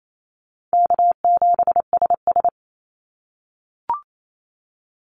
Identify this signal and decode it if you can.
Morse code, heard as a keyed tone, about 28 wpm, 700 Hz. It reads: K7SS
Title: Call Signs - 28wpm